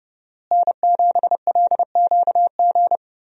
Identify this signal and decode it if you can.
Morse code, heard as a keyed tone, about 29 wpm, 700 Hz. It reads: N7LQG
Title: USA Call Signs - 5 Characters - Repeated 3x - 30wpm